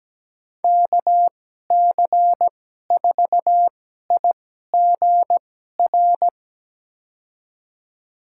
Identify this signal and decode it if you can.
Morse code, heard as a keyed tone, about 17 wpm, spaced as for 12 wpm, 700 Hz. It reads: KC4IGR
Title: USA Call Signs - 6 Characters - 2x Character Spacing - 17wpm